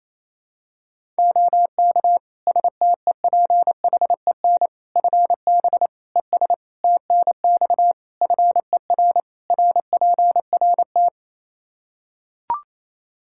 Morse code, heard as a keyed tone, about 27 wpm, 700 Hz. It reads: OK STEPHEN FB ES TNX FER RPRT
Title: K1USN SST QSO Practice 28wpm